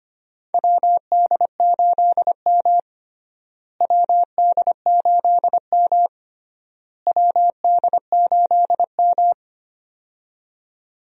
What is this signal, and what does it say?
Morse code, heard as a keyed tone, about 25 wpm, 700 Hz. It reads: WD8M WD8M WD8M
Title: USA Call Signs - 4 Characters - Repeated 3x - 25wpm